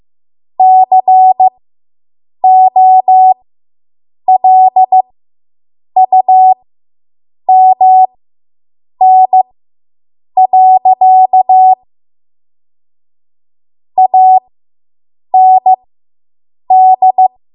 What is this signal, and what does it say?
Morse code, heard as a keyed tone, about 15 wpm, spaced as for 7 wpm, 750 Hz. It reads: COLUMN. AND